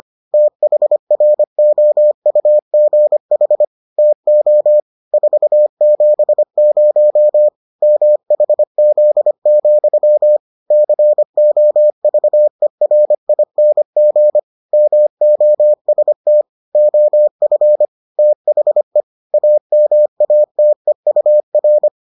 Morse code, heard as a keyed tone, about 25 wpm, 600 Hz. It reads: THROUGH TO 470 MHZ, COVERING MOST OF THE AMATEUR